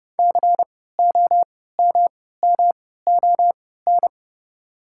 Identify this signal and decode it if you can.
Morse code, heard as a keyed tone, about 30 wpm, spaced as for 17 wpm, 690 Hz. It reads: COMMON
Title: 3x Character Spacing - Top 500 Words - 30wpm